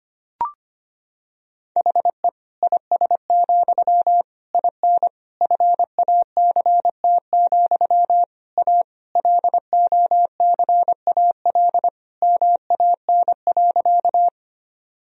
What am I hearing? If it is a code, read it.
Morse code, heard as a keyed tone, about 25 wpm, 700 Hz. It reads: HE IS, IN FACT, A LOCAL MAN.
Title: Sentences from Top 1000 Words - Review Part 3 - 25wpm